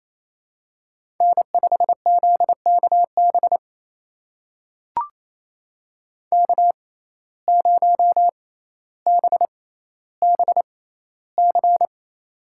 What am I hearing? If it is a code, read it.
Morse code, heard as a keyed tone, about 28 wpm, spaced as for 14 wpm, 700 Hz. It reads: N5ZKB K0BBC
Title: USA Call Signs - 5 Characters - 6x Character Spacing - 28wpm